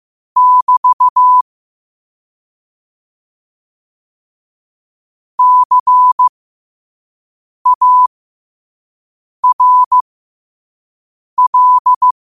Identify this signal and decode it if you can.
Morse code, heard as a keyed tone, about 15 wpm, spaced as for 5 wpm, 1000 Hz. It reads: <BT> CARL
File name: qotdfiles / 5wpm